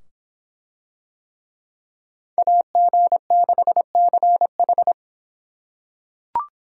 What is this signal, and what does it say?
Morse code, heard as a keyed tone, about 24 wpm, 700 Hz. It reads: AG6CH